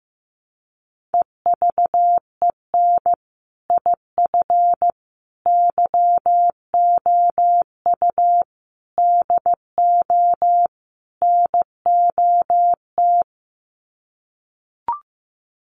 Morse code, heard as a keyed tone, about 15 wpm, 700 Hz. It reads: EVEN IF YOU DO NOT
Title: Sets of 5 Words Encore 15wpm